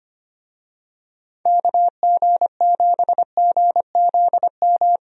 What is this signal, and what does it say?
Morse code, heard as a keyed tone, about 25 wpm, 700 Hz. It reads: KG7GZM
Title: USA Call Signs - 6 Characters - Repeated 2x - 25wpm